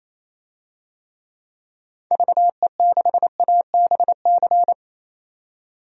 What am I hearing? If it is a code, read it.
Morse code, heard as a keyed tone, about 28 wpm, 700 Hz. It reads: VE6ABC
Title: DXCC-100 Call Signs - 6 Characters - 28wpm